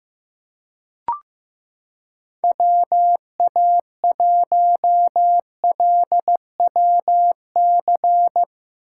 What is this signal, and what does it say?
Morse code, heard as a keyed tone, about 15 wpm, 700 Hz. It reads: WA1LWC
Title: K1USN SST QSO Practice - 15wpm